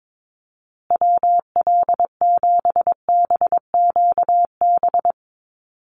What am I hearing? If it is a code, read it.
Morse code, heard as a keyed tone, about 22 wpm, 700 Hz. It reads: WL7BQB
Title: USA Call Signs Repeated 3x 22wpm